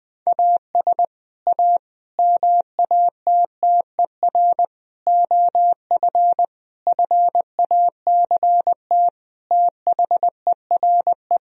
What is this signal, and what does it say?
Morse code, heard as a keyed tone, about 20 wpm, 700 Hz. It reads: AS A MATTER OF FACT THERE
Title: Sentences from Top 1000 Words - Review Part 3 - 20wpm